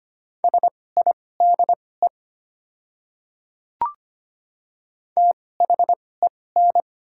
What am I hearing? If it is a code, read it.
Morse code, heard as a keyed tone, about 25 wpm, spaced as for 18 wpm, 700 Hz. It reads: SIDE THEN
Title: 2x Character Spacing - Top 100 Words - 25wpm